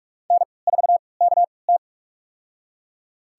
Morse code, heard as a keyed tone, about 46 wpm, spaced as for 25 wpm, 700 Hz. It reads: N4KT
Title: USA Call Signs - 3x Character Spacing - 45wpm